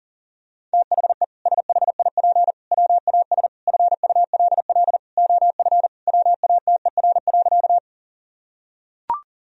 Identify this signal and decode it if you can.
Morse code, heard as a keyed tone, about 40 wpm, 700 Hz. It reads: THE SHIP WAS FULL OF WATER.
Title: Sentences from Top 1000 Words - Review Part 3 - 40wpm